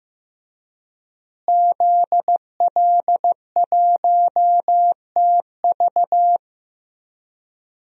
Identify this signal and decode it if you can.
Morse code, heard as a keyed tone, about 16 wpm, 700 Hz. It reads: ZL1TV
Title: DXCC-100 Call Signs - 5 Characters - Repeated 3x - 15wpm